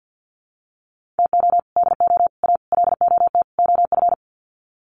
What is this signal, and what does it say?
Morse code, heard as a keyed tone, about 50 wpm, 700 Hz. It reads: TO DO A LOT OF
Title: Sets of 5 Words Encore 50wpm